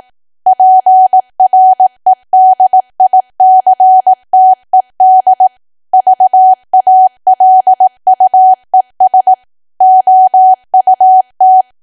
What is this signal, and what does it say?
Morse code, heard as a keyed tone, about 18 wpm, 750 Hz. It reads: PREDICTED VALUES OUT
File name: cwa / W1AW-Level2 / 18WPM